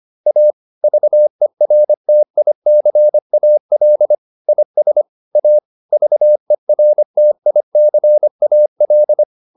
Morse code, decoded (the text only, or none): A VERTICAL IS A VERTICAL